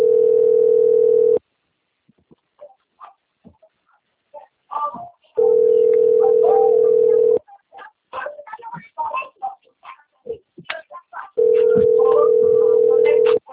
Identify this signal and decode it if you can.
Ringback tone (North America), 440 + 480 Hz, with speech.
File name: rg-1600-unknown- - - .5181